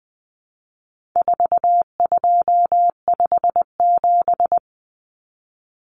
Morse code, heard as a keyed tone, about 20 wpm, 700 Hz. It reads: 4257
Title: Numbers - 4-Digit 20wpm